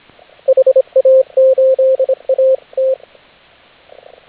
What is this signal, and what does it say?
Morse code, heard as a keyed tone, about 26 wpm, 530 Hz, with radio noise. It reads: HA8AT